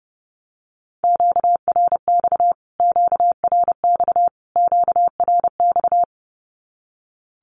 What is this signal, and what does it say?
Morse code, heard as a keyed tone, about 30 wpm, 700 Hz. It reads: QRX QRX QRX
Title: Q-Codes Repeated 3x 30wpm